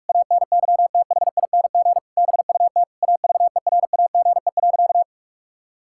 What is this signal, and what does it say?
Morse code, heard as a keyed tone, about 46 wpm, 700 Hz. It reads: ANYTHING BUT AVERAGE.
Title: Sentences from Top 1000 Words - Review Part 6 - 45wpm